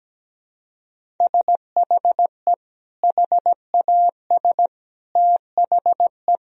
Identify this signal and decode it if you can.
Morse code, heard as a keyed tone, about 17 wpm, 700 Hz. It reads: SHE HAS THE
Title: Sentences from Top 1000 Words - Review Part 3 - 17wpm